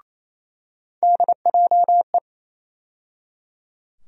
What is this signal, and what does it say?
Morse code, heard as a keyed tone, about 27 wpm, 700 Hz. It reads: DJE
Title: USA Call Sign Postfixes - 3 Letters 28wpm